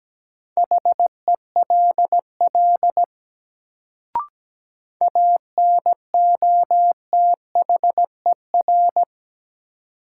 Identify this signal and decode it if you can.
Morse code, heard as a keyed tone, about 17 wpm, 700 Hz. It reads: HELL ANOTHER
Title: Sets of 5 Words Encore - New Word Drill - 17wpm